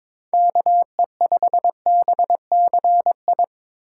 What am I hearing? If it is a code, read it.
Morse code, heard as a keyed tone, about 21 wpm, 700 Hz. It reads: KE5BCI
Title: USA Call Signs - 6 Characters - 4x Character Spacing - 22wpm